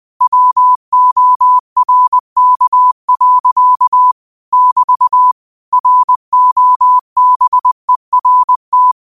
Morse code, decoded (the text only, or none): WORK. <BT> ROBERT